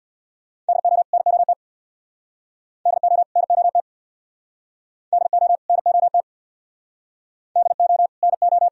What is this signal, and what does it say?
Morse code, heard as a keyed tone, about 68 wpm, 700 Hz. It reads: DO NOT DO NOT DO NOT DO NO